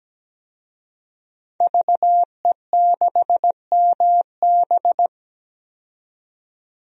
Morse code, heard as a keyed tone, about 17 wpm, 700 Hz. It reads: VE6MB
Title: DXCC-100 Call Signs - 5 Characters - 17wpm